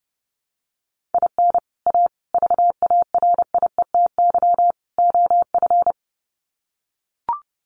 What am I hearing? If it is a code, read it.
Morse code, heard as a keyed tone, about 30 wpm, 700 Hz. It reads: IN A VARIETY OF
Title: Sets of 4 Words 30wpm